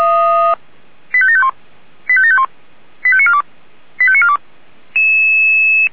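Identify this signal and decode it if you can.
Five-tone selective call (ZVEI), heard as quick runs of five tones, tones of about 70 ms. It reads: 85751 85751 85942 85942
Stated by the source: ZVEI five-tone selective call 85730, received by SDR.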